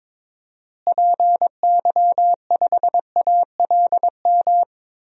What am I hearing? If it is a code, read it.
Morse code, heard as a keyed tone, about 22 wpm, 700 Hz. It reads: PY5ALM
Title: DXCC-100 Call Signs - 6 Characters - Repeated 3x - 22wpm